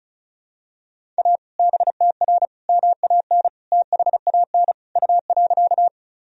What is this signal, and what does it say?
Morse code, heard as a keyed tone, about 34 wpm, 700 Hz. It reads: A BTR MAN THAN U.
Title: Sentences from Top 400 Words and Abbreviations 35wpm